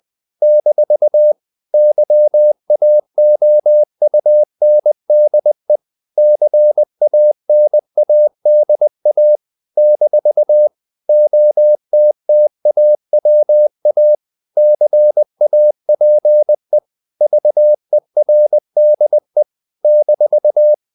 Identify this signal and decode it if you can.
Morse code, heard as a keyed tone, about 20 wpm, 600 Hz. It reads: - YAOUNDE CANADA - OTTAWA CAPE VERDE -
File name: CW / Practice / NationalCapitals20